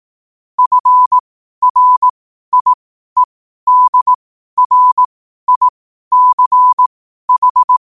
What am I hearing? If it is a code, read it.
Morse code, heard as a keyed tone, about 18 wpm, spaced as for 12 wpm, 1000 Hz. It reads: FRIEDRICH
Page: Quotes for Sun, 27 Apr 2025 in Morse Code at 12 words per minute.